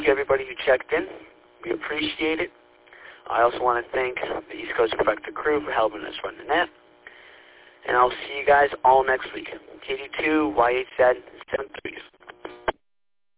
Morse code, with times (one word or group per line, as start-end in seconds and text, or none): none